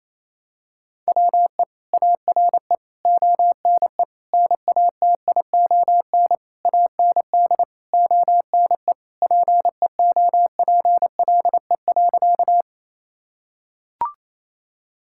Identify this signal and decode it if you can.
Morse code, heard as a keyed tone, about 28 wpm, 700 Hz. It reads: WE ARE ONE NATION AND ONE PEOPLE.
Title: Sentences from Top 1000 Words - Review Part 6 - 28wpm